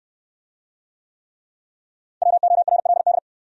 Morse code, heard as a keyed tone, about 58 wpm, 700 Hz. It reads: WORLD